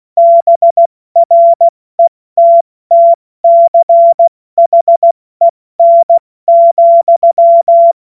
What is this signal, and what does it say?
Morse code, heard as a keyed tone, about 16 wpm, spaced as for 14 wpm, 680 Hz. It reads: BRETTCHEN,
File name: Morsekurs / Witze / Witz 20. Januar 2026 70 F80